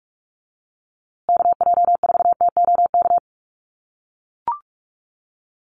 Morse code, heard as a keyed tone, about 44 wpm, 690 Hz. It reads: KJ4TOK
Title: USA Call Signs Repeated 3x 45wpm